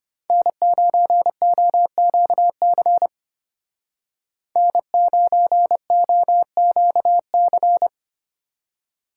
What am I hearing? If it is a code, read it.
Morse code, heard as a keyed tone, about 31 wpm, 700 Hz. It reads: N9OQC N9OQC
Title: Speed Racing - USA Call Signs 25wpm